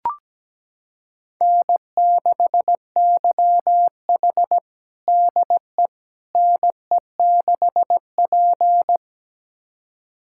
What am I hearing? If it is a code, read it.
Morse code, heard as a keyed tone, about 17 wpm, 700 Hz. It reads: N6YH DE NE6P
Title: USA Callsign DE Callsign - 4 Characters - 17wpm